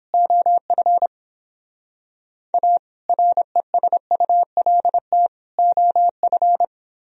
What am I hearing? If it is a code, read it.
Morse code, heard as a keyed tone, about 29 wpm, 700 Hz. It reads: OF A RESULT OF